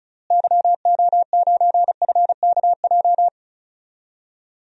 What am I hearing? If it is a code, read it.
Morse code, heard as a keyed tone, about 34 wpm, 700 Hz. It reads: YO9FKJ
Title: DXCC-100 Call Signs - 6 Characters - Repeated 2x - 35wpm